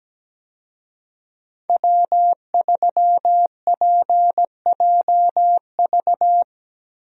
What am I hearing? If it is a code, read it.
Morse code, heard as a keyed tone, about 17 wpm, 700 Hz. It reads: W3PJV
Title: USA Call Signs - 5 Characters - Repeated 3x - 17wpm